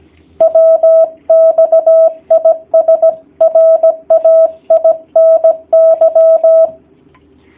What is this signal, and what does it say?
Morse code, heard as a keyed tone, about 17 wpm, 650 Hz. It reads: WX IS RAINY
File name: jh8cbh / 2019cw / sp16- w